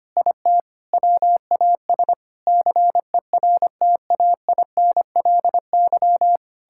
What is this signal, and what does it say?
Morse code, heard as a keyed tone, about 25 wpm, 700 Hz. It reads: IT WAS CERTAINLY
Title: Sentences from Top 1000 Words - Review Part 6 - 25wpm